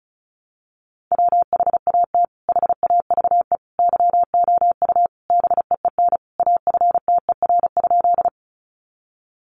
Morse code, heard as a keyed tone, about 34 wpm, 700 Hz. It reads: WHAT HAVE YOU BEEN AFTER?
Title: Sentences from Top 200 Words 35wpm